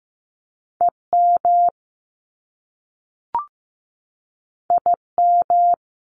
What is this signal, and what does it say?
Morse code, heard as a keyed tone, about 15 wpm, 700 Hz. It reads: EM IM
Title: Top Two Letter Combinations in Words 15wpm